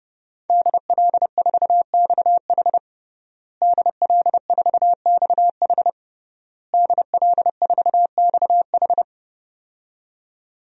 Morse code, heard as a keyed tone, about 30 wpm, 700 Hz. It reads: DL4XH DL4XH DL4XH
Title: DXCC-100 Call Signs - 5 Characters - Repeated 3x - 30wpm